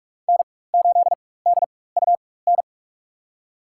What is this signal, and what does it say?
Morse code, heard as a keyed tone, about 43 wpm, spaced as for 21 wpm, 700 Hz. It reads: N8DUN
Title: USA Call Signs - 4x Character Spacing - 45wpm